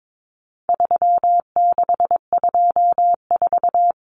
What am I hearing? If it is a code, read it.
Morse code, heard as a keyed tone, about 22 wpm, 700 Hz. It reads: 3624
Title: Common Cut Numbers - 4-Digit 22wpm